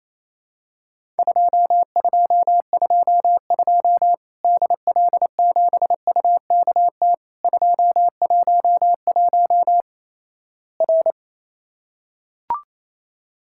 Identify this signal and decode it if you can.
Morse code, heard as a keyed tone, about 27 wpm, 700 Hz. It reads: 2222 DL7UKT 211 R
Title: Worked All Europe DX Contest - QTC Exchange - 28wpm